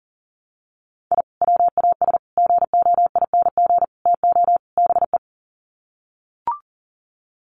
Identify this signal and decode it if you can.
Morse code, heard as a keyed tone, about 40 wpm, 700 Hz. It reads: I WAS GOING TO BE
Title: Sets of 5 Words 40wpm